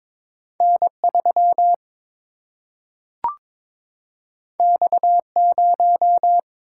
Morse code, heard as a keyed tone, about 22 wpm, 700 Hz. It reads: N3 X0
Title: Single-Letter Single-Number 22wpm